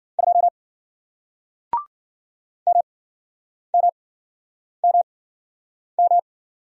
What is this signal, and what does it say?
Morse code, heard as a keyed tone, about 52 wpm, spaced as for 9 wpm, 700 Hz. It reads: 2 MMMM